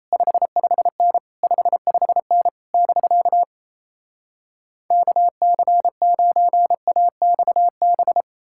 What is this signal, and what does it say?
Morse code, heard as a keyed tone, about 38 wpm, 700 Hz. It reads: 55N 55N <BK> KC9AXB